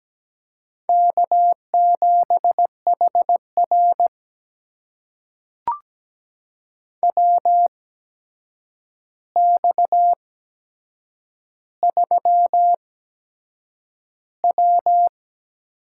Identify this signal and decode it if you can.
Morse code, heard as a keyed tone, about 17 wpm, spaced as for 7 wpm, 700 Hz. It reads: K7HR WX3W
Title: USA Call Signs - 4 Characters - 8x Character Spacing - 17wpm